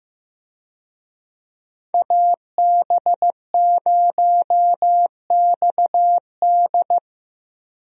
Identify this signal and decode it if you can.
Morse code, heard as a keyed tone, about 15 wpm, 700 Hz. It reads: AB0XD
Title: USA Call Signs - 5 Characters - 15wpm